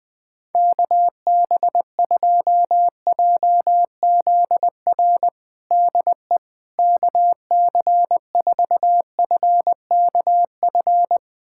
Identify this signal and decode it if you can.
Morse code, heard as a keyed tone, about 20 wpm, 700 Hz. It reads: KB2JZR DE KC4FKF
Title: USA Callsign DE Callsign - 6 Characters - 20wpm